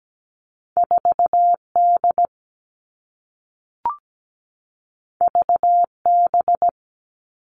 Morse code, heard as a keyed tone, about 17 wpm, 700 Hz. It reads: 4D VB
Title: All Cut Numbers - 2-Digit 17wpm